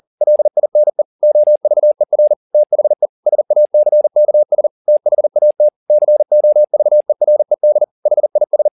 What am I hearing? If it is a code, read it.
Morse code, heard as a keyed tone, about 40 wpm, 600 Hz. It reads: LINE OVER THE SACKS THAT COVERED HIS